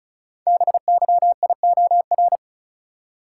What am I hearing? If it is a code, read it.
Morse code, heard as a keyed tone, about 36 wpm, 700 Hz. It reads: BYIOR
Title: Code Groups - 5 Letters - 35wpm